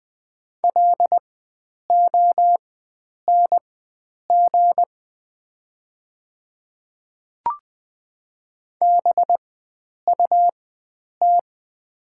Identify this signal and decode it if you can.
Morse code, heard as a keyed tone, about 20 wpm, spaced as for 9 wpm, 700 Hz. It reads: LONG BUT
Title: 4x Character Spacing - Top 100 Words - 20wpm